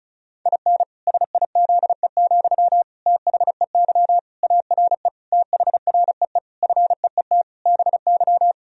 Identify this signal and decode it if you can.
Morse code, heard as a keyed tone, about 36 wpm, 700 Hz. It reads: IN SIZE, THEY ARE THREE FEET BY
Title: Sentences from Top 1000 Words - Review Part 2 - 35wpm